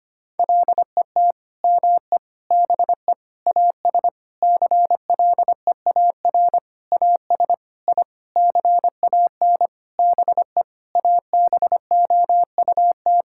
Morse code, heard as a keyed tone, about 25 wpm, 700 Hz. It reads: LET ME BE AS CLEAR AS I CAN BE ABOUT
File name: practice / 25wpm / Sentences with 9-Word Phrases - 25wpm